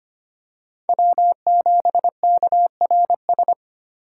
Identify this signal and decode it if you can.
Morse code, heard as a keyed tone, about 25 wpm, 700 Hz. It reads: W7KRS
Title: K1USN SST QSO Practice - 25wpm